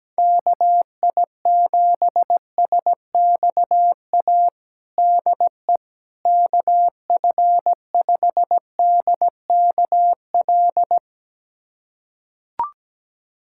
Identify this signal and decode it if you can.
Morse code, heard as a keyed tone, about 17 wpm, 700 Hz. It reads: KI7SXA DE KF5DKL
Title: USA Callsign DE Callsign - 6 Characters - 17wpm